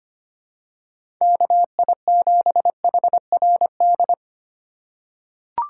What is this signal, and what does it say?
Morse code, heard as a keyed tone, about 25 wpm, 700 Hz. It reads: KI7HRD
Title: USA Call Signs - 2x Character Spacing - 25wpm